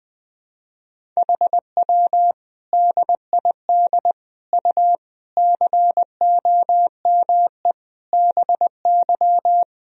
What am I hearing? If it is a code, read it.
Morse code, heard as a keyed tone, about 21 wpm, 700 Hz. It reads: HW DID U COME BY
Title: Sentences from Top 100 Words and Abbreviations 20wpm